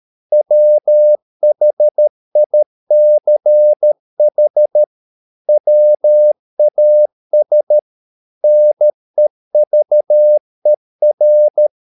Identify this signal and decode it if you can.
Morse code, heard as a keyed tone, about 13 wpm, 600 Hz. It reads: WHICH WAS NEVER